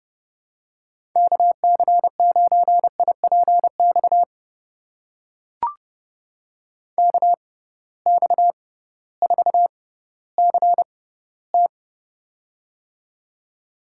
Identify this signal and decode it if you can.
Morse code, heard as a keyed tone, about 29 wpm, spaced as for 16 wpm, 700 Hz. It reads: KC9IPX KX4CT
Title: USA Call Signs - 6x Character Spacing - 30wpm